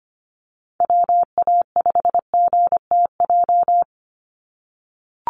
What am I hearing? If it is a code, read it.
Morse code, heard as a keyed tone, about 25 wpm, 700 Hz. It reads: WA5GTJ
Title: USA Call Signs 25wpm